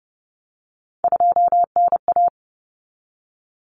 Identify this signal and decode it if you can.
Morse code, heard as a keyed tone, about 30 wpm, 700 Hz. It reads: 2NA
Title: Common Cut Numbers - 3-Digit 30wpm